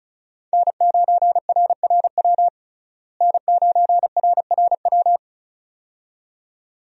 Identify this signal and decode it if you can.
Morse code, heard as a keyed tone, about 34 wpm, 700 Hz. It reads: N9RRW N9RRW
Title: USA Call Signs - 5 Characters - Repeated 2x - 35wpm